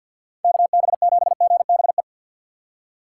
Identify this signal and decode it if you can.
Morse code, heard as a keyed tone, about 47 wpm, 700 Hz. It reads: KB7GBE
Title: USA Call Signs - 6 Characters - 3x Character Spacing - 50wpm